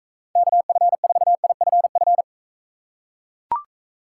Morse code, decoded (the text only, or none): KF4IFF